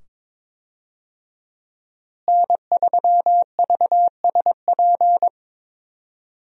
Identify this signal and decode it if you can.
Morse code, heard as a keyed tone, about 22 wpm, 700 Hz. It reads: N3VSP
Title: USA Call Signs - 5 Characters - Repeated 2x - 22wpm